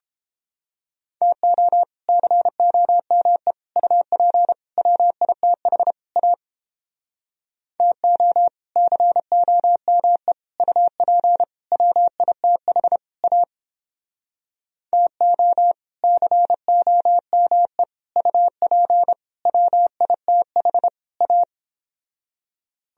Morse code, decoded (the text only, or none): TO COME UP WITH A TO COME UP WITH A TO COME UP WITH A